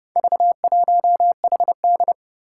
Morse code, heard as a keyed tone, about 29 wpm, 700 Hz. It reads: V1HD
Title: Four Challenging Letter-Number Combinations 30wpm